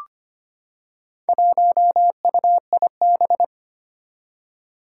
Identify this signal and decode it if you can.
Morse code, heard as a keyed tone, about 25 wpm, 700 Hz. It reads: 1UIB
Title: Four Challenging Letter-Number Combinations 25wpm